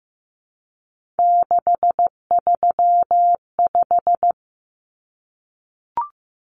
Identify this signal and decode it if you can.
Morse code, heard as a keyed tone, about 15 wpm, 700 Hz. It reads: 635
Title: Common Cut Numbers - 3-Digit 15wpm